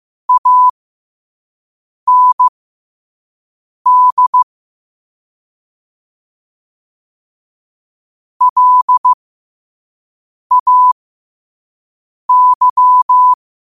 Morse code, heard as a keyed tone, about 15 wpm, spaced as for 5 wpm, 1000 Hz. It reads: AND LAY
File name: qotdfiles / 5wpm